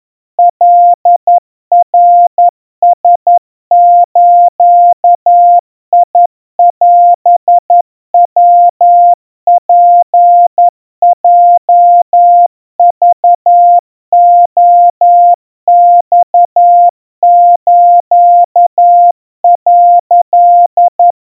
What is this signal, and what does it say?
Morse code (Japanese Wabun code), heard as a keyed tone, about 11 wpm, 700 Hz. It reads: カナラズオヤツヲクレマス」
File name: kawa / Morse sample / kamyu 40 cpm 700 Hz Crispy